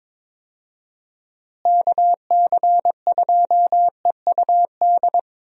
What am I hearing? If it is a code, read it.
Morse code, heard as a keyed tone, about 22 wpm, 700 Hz. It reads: KC2EUD
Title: USA Call Signs - 6 Characters - Repeated 2x - 22wpm